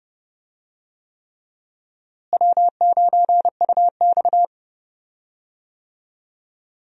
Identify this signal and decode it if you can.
Morse code, heard as a keyed tone, about 30 wpm, 700 Hz. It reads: W9UX
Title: USA Call Signs - 4 Characters - Repeated 3x - 30wpm